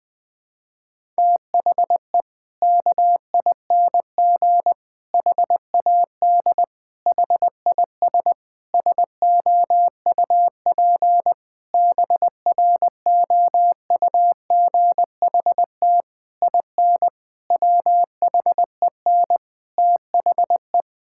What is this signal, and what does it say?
Morse code, heard as a keyed tone, about 20 wpm, 700 Hz. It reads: THE KING HAD HIS SOUP BROUGHT IN WHEN THE
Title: Sentences from Top 1000 Words - Review Part 6 - 20wpm